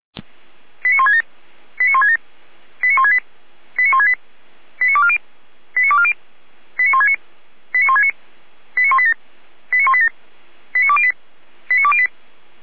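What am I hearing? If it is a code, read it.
Five-tone selective call (ZVEI), heard as quick runs of five tones, tones of about 80 ms. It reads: 89167 89167 89168 89168 89240 89240 89169 89169 89187 89187 89208 89208
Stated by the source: ZVEI five-tone selective call received by SDR.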